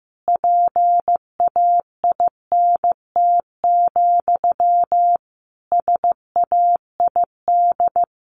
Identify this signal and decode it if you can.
Morse code, heard as a keyed tone, about 15 wpm, 700 Hz. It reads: PAINT, SAID
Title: A Sound Of Thunder 15wpm